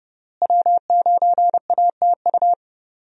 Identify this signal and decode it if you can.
Morse code, heard as a keyed tone, about 30 wpm, 700 Hz. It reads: W9ATU
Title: USA Call Signs - 5 Characters - Repeated 3x - 30wpm